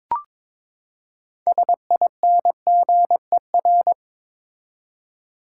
Morse code, heard as a keyed tone, about 22 wpm, 700 Hz. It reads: SINGER
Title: Six Letter Words - 22wpm